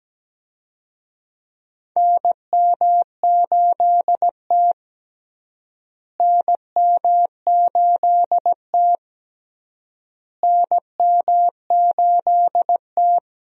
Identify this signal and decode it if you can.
Morse code, heard as a keyed tone, about 17 wpm, 700 Hz. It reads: NM8T NM8T NM8T
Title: USA Call Signs - 4 Characters - Repeated 3x - 17wpm